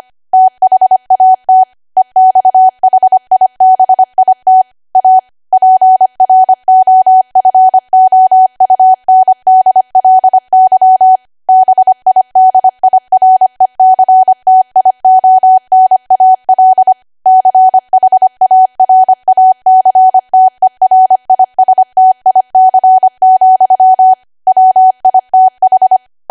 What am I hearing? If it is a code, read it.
Morse code, heard as a keyed tone, about 25 wpm, 750 Hz. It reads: THAT EXHIBIT A PROFOUNDLY BIDIRECTIONAL CHARACTERISTIC, WITH
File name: Archive / 25 WPM / 25WPM